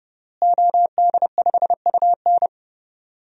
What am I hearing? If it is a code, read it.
Morse code, heard as a keyed tone, about 30 wpm, 700 Hz. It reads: OD5UN
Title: DXCC-100 Call Signs - 5 Characters - Repeated 3x - 30wpm